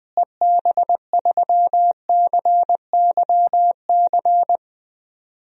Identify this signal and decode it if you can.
Morse code, heard as a keyed tone, about 20 wpm, 700 Hz. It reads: EB3CYC
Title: DXCC-100 Call Signs - 6 Characters - Repeated 3x - 20wpm